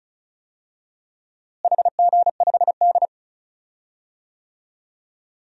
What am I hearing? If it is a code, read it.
Morse code, heard as a keyed tone, about 36 wpm, 700 Hz. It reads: HG5D
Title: DXCC-100 Call Signs - 4 Characters - Repeated 3x - 35wpm